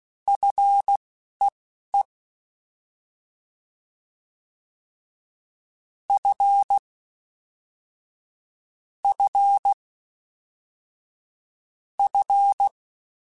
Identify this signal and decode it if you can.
Morse code, heard as a keyed tone, about 16 wpm, spaced as for 11 wpm, 800 Hz. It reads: FEE F F F
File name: F and L / F and L mp3 / F and L Prefixes 3x2 Twice @10